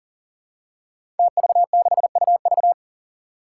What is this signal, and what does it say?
Morse code, heard as a keyed tone, about 40 wpm, 700 Hz. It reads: TV6UV
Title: Five Challenging Letter-Number Combinations 40wpm